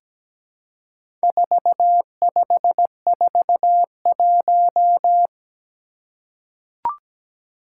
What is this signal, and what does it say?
Morse code, heard as a keyed tone, about 17 wpm, 700 Hz. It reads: 4541
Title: Four Challenging Letter-Number Combinations 17wpm